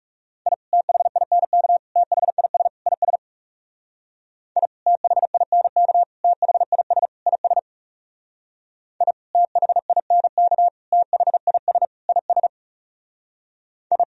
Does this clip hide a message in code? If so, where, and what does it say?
Yes: Morse code from 0.5 to 14.0 s, heard as a keyed tone, about 40 wpm, 700 Hz. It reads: I THINK THIS IS I THINK THIS IS I THINK THIS IS I